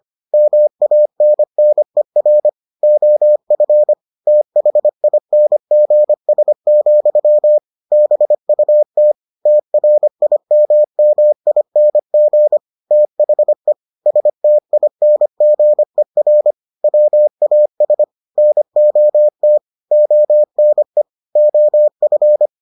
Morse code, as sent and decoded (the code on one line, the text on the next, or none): -- .- -. -. .-. / --- ..-. / - .... .. -. --. ... --..-- / -... ..- - / - .-. .. -- -- .. -. --. / - .... / ... - .. -. --. .-. / .-- .- ... / -. --- - / --- -. / --- ..-.
MANNER OF THINGS, BUT TRIMMING THE STINGER WAS NOT ONE OF